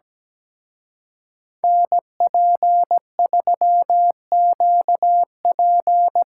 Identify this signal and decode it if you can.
Morse code, heard as a keyed tone, about 20 wpm, 700 Hz. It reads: NP3QP
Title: Speed Racing - USA 2x2 Call Signs 17wpm